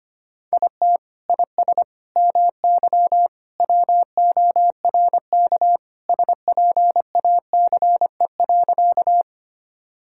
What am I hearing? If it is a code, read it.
Morse code, heard as a keyed tone, about 25 wpm, 700 Hz. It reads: IT IS MY WORK SPACE.
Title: Sentences from Top 1000 Words - Review Part 5 - 25wpm